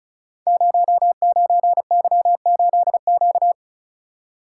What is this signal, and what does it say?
Morse code, heard as a keyed tone, about 36 wpm, 700 Hz. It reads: 09Y8Q
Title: Code Groups - 5 Letter:Numbers - 35wpm